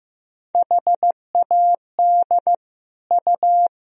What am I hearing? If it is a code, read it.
Morse code, heard as a keyed tone, about 16 wpm, 700 Hz. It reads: HAD U
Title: Sentences from Top 400 Words and Abbreviations 15wpm